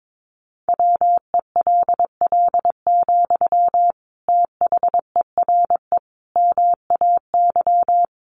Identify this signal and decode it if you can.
Morse code, heard as a keyed tone, about 22 wpm, 700 Hz. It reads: WELL, THERE MAY
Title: Sentences from Top 1300 Words 22wpm